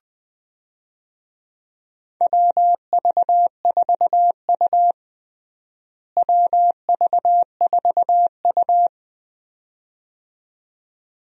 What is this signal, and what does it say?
Morse code, heard as a keyed tone, about 20 wpm, 700 Hz. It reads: WV4U WV4U
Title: USA Call Signs - 4 Characters - Repeated 2x - 20wpm